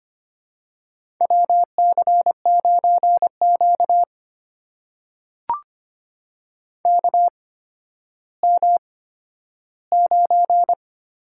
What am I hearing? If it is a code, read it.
Morse code, heard as a keyed tone, about 25 wpm, spaced as for 12 wpm, 700 Hz. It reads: WC9Q KM9
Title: USA Call Signs - 4 Characters - 8x Character Spacing - 25wpm